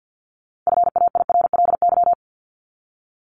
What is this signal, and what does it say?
Morse code, heard as a keyed tone, about 50 wpm, 700 Hz. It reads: FAIRLY
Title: Top 1001-1250 Words 50wpm